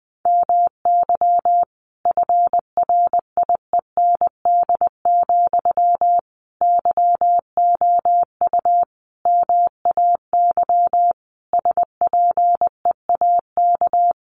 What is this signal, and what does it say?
Morse code, heard as a keyed tone, about 20 wpm, 700 Hz. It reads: MY FRIEND, YOU MAY SPEAK
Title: Sentences from Top 1300 Words 20wpm